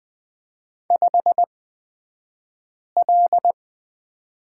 Morse code, heard as a keyed tone, about 20 wpm, spaced as for 5 wpm, 700 Hz. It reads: 5L
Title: 249 Lesson 40d - BK - Two Characters 20wpm